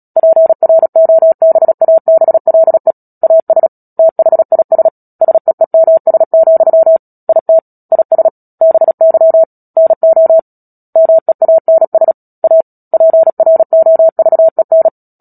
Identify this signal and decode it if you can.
Morse code, heard as a keyed tone, about 37 wpm, 650 Hz. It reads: PROBABLE AS THIS SEEMS, IT IS BY NO MEANS A PROVEN
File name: Book Two / 10-Chapter 10 - Epilogue - 36 WPM